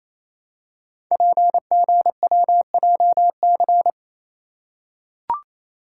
Morse code, heard as a keyed tone, about 27 wpm, 700 Hz. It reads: PGWJC
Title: Code Groups - 5 Letters - 28wpm